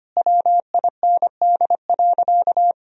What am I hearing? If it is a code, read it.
Morse code, heard as a keyed tone, about 25 wpm, 700 Hz. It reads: WIND.
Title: Sentences from Top 1000 Words - Review Part 2 - 25wpm